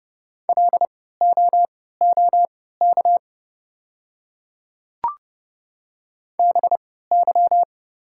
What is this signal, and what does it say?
Morse code, heard as a keyed tone, about 30 wpm, spaced as for 17 wpm, 700 Hz. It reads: LOOK BY
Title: 3x Character Spacing - Top 300 Words - 30wpm